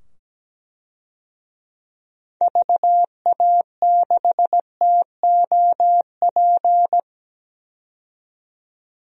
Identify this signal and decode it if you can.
Morse code, heard as a keyed tone, about 17 wpm, 700 Hz. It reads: VA6TOP
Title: DXCC-100 Call Signs - 6 Characters - 17wpm